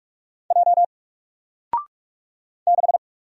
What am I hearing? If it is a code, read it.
Morse code, heard as a keyed tone, about 44 wpm, 700 Hz. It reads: J 6
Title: Single Letter-Number 45wpm